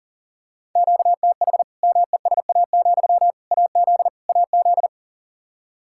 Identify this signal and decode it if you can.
Morse code, heard as a keyed tone, about 40 wpm, 700 Hz. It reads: QTH MESA, AZ AZ
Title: Practice QSO - Exchange 1 - Beginning - 40wpm